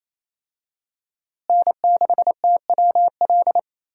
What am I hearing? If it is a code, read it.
Morse code, heard as a keyed tone, about 28 wpm, 700 Hz. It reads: N6TWL
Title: USA Call Signs - 5 Characters - Repeated 3x - 28wpm